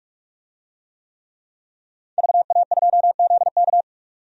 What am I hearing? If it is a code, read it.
Morse code, heard as a keyed tone, about 46 wpm, 700 Hz. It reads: VA2ZK
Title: DXCC-100 Call Signs - 5 Characters - 45wpm